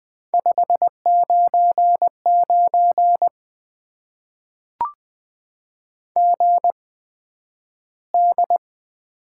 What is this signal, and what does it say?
Morse code, heard as a keyed tone, about 20 wpm, spaced as for 11 wpm, 700 Hz. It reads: 599 GD
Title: Top 90 QSO Elements - 8x Character Spacing - 20wpm